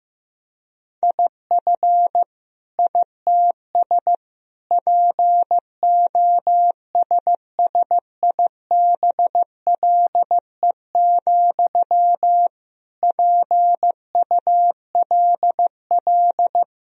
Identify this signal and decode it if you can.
Morse code, heard as a keyed tone, about 15 wpm, 700 Hz. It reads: IF ITS POSSIBLE, PULL
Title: Sentences from Top 1000 Words - Review Part 2 - 15wpm